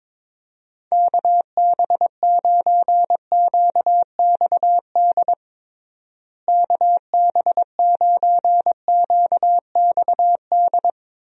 Morse code, heard as a keyed tone, about 21 wpm, 700 Hz. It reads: KB9QXD KB9QXD
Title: USA Call Signs - 6 Characters - Repeated 3x - 22wpm